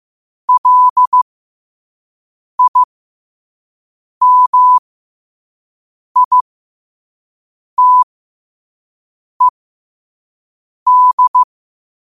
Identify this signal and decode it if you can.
Morse code, heard as a keyed tone, about 15 wpm, spaced as for 5 wpm, 1000 Hz. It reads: LIMITED
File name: qotdfiles / 5wpm